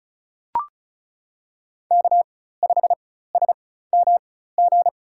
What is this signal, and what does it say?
Morse code, heard as a keyed tone, about 34 wpm, spaced as for 16 wpm, 700 Hz. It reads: K5SMG
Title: USA Call Signs - 5 Characters - 4x Character Spacing - 35wpm